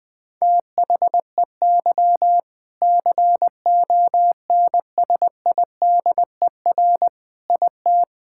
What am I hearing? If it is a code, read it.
Morse code, heard as a keyed tone, about 21 wpm, 700 Hz. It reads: THEY CONSIDER IT
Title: Sentences from Top 1000 Words - Review Part 5 - 20wpm